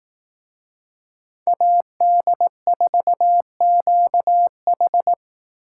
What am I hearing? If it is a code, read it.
Morse code, heard as a keyed tone, about 18 wpm, 700 Hz. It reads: AD4QH
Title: Speed Racing - USA Call Signs 15wpm